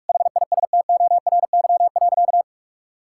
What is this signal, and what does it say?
Morse code, heard as a keyed tone, about 41 wpm, 690 Hz. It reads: HISTORY.
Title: Sentences from Top 1000 Words - Review Part 5 - 45wpm